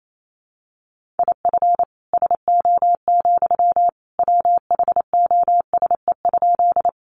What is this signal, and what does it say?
Morse code, heard as a keyed tone, about 28 wpm, 700 Hz. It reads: IF SO, WHOSE?
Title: Sentences from Top 1200 Words 28wpm